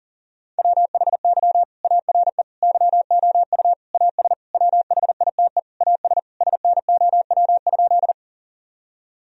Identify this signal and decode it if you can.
Morse code, heard as a keyed tone, about 37 wpm, 700 Hz. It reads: WHY ARE YOU AS WHITE AS SNOW?
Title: Sentences from Top 1000 Words - Review Part 3 - 40wpm